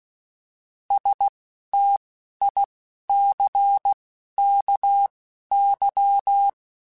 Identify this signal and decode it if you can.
Morse code, heard as a keyed tone, about 16 wpm, spaced as for 11 wpm, 800 Hz. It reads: STICKY
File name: Misc / Misc-mp3s / Word Pairs @10